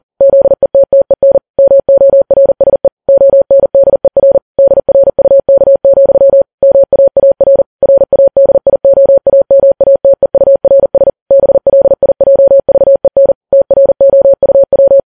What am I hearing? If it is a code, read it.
Morse code, heard as a keyed tone, about 40 wpm, 550 Hz. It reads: ZETTEN MORSE ONDER DRUK, MAAR RADIOAMATEURS BLIJVEN TROUW
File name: Files Web tls / Morse on9fv / morse40kps